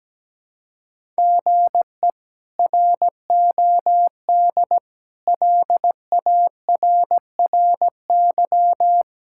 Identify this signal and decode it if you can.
Morse code, heard as a keyed tone, about 17 wpm, 700 Hz. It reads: GE ROD LARRY
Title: K1USN SST QSO Practice - 17wpm